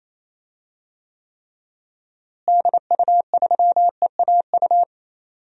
Morse code, heard as a keyed tone, about 28 wpm, 700 Hz. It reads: DU3EAU
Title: DXCC-100 Call Signs - 6 Characters - Repeated 2x - 28wpm